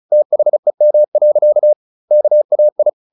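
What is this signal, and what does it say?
Morse code, heard as a keyed tone, about 34 wpm, 600 Hz. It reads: THEM. KAI
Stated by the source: Morse practice at 35 wpm from CW med Gnister, sending Ken Follett's novel Never in English.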